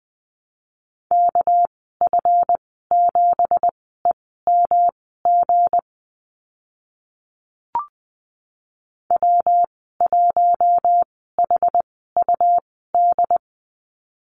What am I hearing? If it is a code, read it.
Morse code, heard as a keyed tone, about 20 wpm, spaced as for 14 wpm, 700 Hz. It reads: KF7EMG W1HUD
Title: 219 Lesson 35g - Number 0 - Call Signs 20wpm